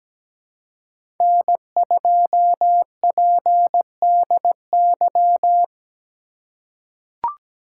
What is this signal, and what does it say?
Morse code, heard as a keyed tone, about 17 wpm, 700 Hz. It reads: N2PDY
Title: USA Call Signs - 5 Characters - 2x Character Spacing - 17wpm